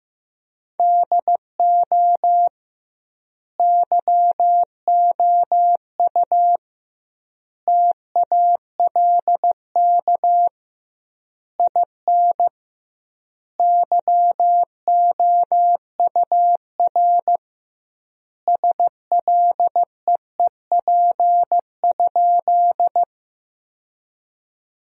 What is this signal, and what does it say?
Morse code, heard as a keyed tone, about 15 wpm, 700 Hz. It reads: DO YOU TALK IN YOUR SLEEP?
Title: 2x Word-Spacing - Sentences from Top 500 Words - 15wpm